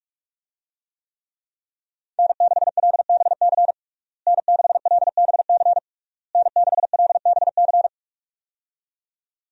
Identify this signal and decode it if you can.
Morse code, heard as a keyed tone, about 42 wpm, 700 Hz. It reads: N6LBC N6LBC N6LBC
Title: USA Call Signs - 5 Characters - Repeated 3x - 45wpm